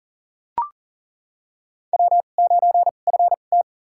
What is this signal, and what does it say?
Morse code, heard as a keyed tone, about 37 wpm, spaced as for 28 wpm, 700 Hz. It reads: W9FT
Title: USA Call Signs - 4 Characters - 2x Character Spacing - 40wpm